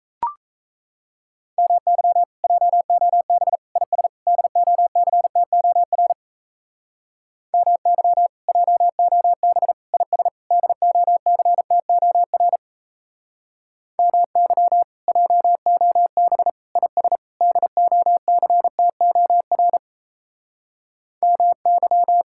Morse code, heard as a keyed tone, about 34 wpm, 700 Hz. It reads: MY JOB IS DOCTOR MY JOB IS DOCTOR MY JOB IS DOCTOR MY
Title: Speed-Racing Common QSO Phrases - 28wpm